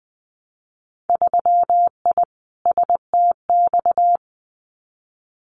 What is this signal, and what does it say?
Morse code, heard as a keyed tone, about 20 wpm, 700 Hz. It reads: 3I STX
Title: Winter Field Day Exchanges Encore 20wpm